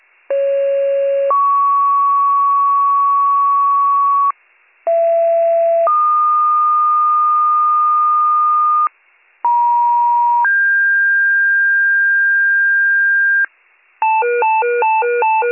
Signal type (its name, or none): two-tone page